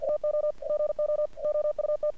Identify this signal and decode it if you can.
Morse code, heard as a keyed tone, about 48 wpm, 600 Hz. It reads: TO GO OUT
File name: archived / CRSnet